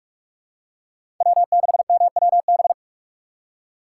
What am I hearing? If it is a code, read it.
Morse code, heard as a keyed tone, about 46 wpm, 700 Hz. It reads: W6MWB
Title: USA Call Signs - 5 Characters - 4x Character Spacing - 45wpm